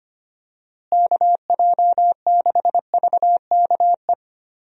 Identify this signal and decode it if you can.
Morse code, heard as a keyed tone, about 25 wpm, 700 Hz. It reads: KJ6VKE
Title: USA Call Signs - 6 Characters - Repeated 3x - 25wpm